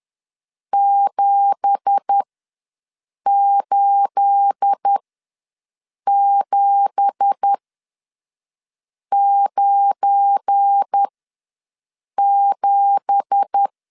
Morse code, heard as a keyed tone, about 10 wpm, spaced as for 5 wpm, 800 Hz. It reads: 78797
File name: LEZIONI / 9 2867 ws 10wpm 4215